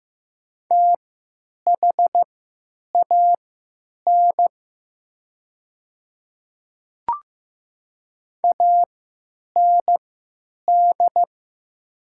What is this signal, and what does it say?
Morse code, heard as a keyed tone, about 15 wpm, spaced as for 9 wpm, 700 Hz. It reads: THAN AND
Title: 3x Character Spacing - Top 100 Words - 15wpm